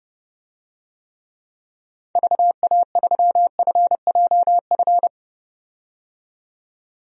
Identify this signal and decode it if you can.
Morse code, heard as a keyed tone, about 31 wpm, 700 Hz. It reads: VA3FJF
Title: DXCC-100 Call Signs - 6 Characters - 30wpm